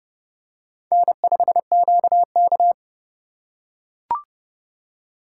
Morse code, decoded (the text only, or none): N5QK